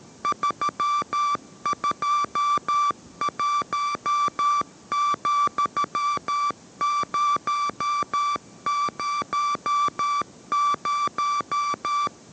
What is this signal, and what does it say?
Morse code, heard as a keyed tone, about 13 wpm, 1170 Hz. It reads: 321,000